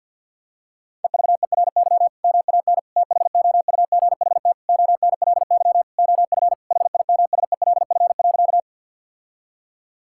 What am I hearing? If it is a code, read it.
Morse code, heard as a keyed tone, about 50 wpm, 700 Hz. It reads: EVERY MAN THOUGHT ONLY OF HIMSELF.
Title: Sentences from Top 1600 Words 50wpm